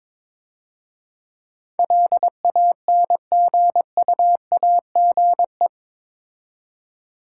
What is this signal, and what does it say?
Morse code, heard as a keyed tone, about 26 wpm, 700 Hz. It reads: LANGUAGE
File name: Easy Words - Speed-Racing with Hints - 22wpm